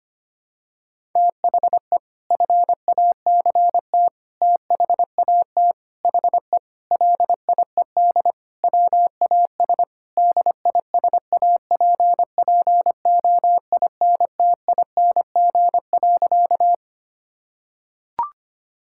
Morse code, heard as a keyed tone, about 25 wpm, 700 Hz. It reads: THE FACT THAT HE LIED WAS DISAPPOINTING.
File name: practice / 25wpm / Sentences with 4-Word Phrases - 25wpm